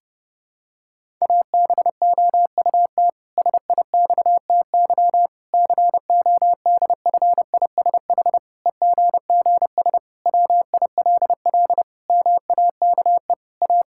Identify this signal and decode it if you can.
Morse code, heard as a keyed tone, about 30 wpm, 700 Hz. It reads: ABOUT SIXTY CODFISH EGGS WILL MAKE A
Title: Pangram Challenge v2 - 30wpm